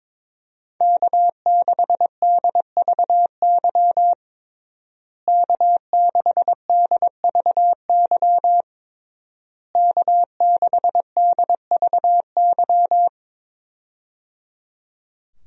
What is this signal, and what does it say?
Morse code, heard as a keyed tone, about 22 wpm, 700 Hz. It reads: K6DVY K6DVY K6DVY
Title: USA Call Signs - 5 Characters - Repeated 3x - 22wpm